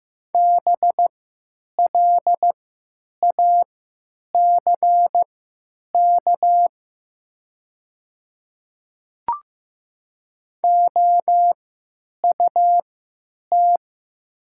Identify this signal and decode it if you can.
Morse code, heard as a keyed tone, about 16 wpm, spaced as for 9 wpm, 700 Hz. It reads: BLACK OUT
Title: 3x Character Spacing - Top 500 Words - 15wpm